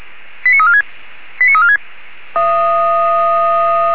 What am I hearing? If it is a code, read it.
Five-tone selective call (ZVEI), heard as quick runs of five tones, tones of about 80 ms. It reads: 89346 89346
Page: SDR/ZVEI/89315